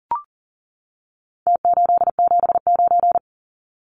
Morse code, heard as a keyed tone, about 37 wpm, 700 Hz. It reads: T879
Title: Numbers - Contest Style - 4-Digit 40wpm